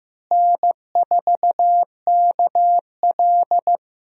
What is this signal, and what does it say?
Morse code, heard as a keyed tone, about 15 wpm, 700 Hz. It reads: N4KL
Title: USA Call Signs - 4 Characters - Repeated 2x - 15wpm